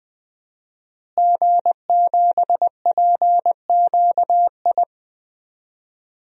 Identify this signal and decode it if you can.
Morse code, heard as a keyed tone, about 21 wpm, 700 Hz. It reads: G7PQI
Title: DXCC-100 Call Signs - 5 Characters - Repeated 2x - 20wpm